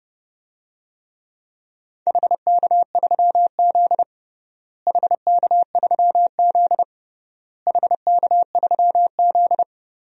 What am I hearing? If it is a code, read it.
Morse code, heard as a keyed tone, about 32 wpm, 700 Hz. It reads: HK3Z HK3Z HK3Z
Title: DXCC-100 Call Signs - 4 Characters - Repeated 3x - 30wpm